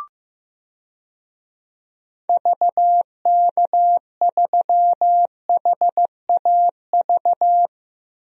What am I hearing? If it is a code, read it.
Morse code, heard as a keyed tone, about 15 wpm, 700 Hz. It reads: VK3HAV
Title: DXCC-100 Call Signs - 6 Characters - Repeated 2x - 15wpm